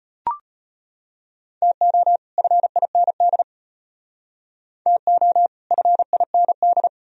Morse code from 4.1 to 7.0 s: TO FIND

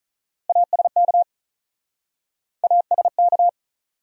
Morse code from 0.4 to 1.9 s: ASK

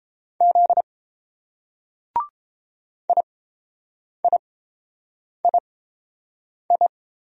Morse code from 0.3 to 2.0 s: Z

